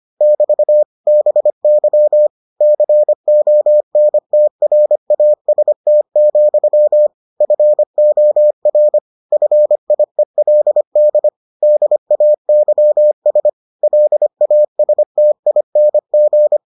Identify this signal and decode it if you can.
Morse code, heard as a keyed tone, about 25 wpm, 600 Hz. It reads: <BT> BY CONTRAST, FOR FIELD DAYS LASTING